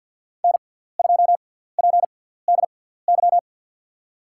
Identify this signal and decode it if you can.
Morse code, heard as a keyed tone, about 53 wpm, 700 Hz. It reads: N 2 P D Y
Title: USA Call Signs - 6x Character Spacing - 50wpm